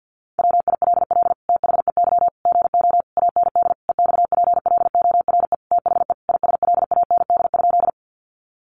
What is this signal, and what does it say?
Morse code, heard as a keyed tone, about 50 wpm, 700 Hz. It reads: WILL THEY GO AND EXPLORE THE ISLAND?
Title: Sentences from Top 1100 Words 50wpm